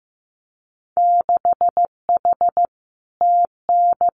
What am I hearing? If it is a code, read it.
Morse code, heard as a keyed tone, about 15 wpm, 700 Hz. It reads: 6H TN
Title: Winter Field Day Exchanges Encore 15wpm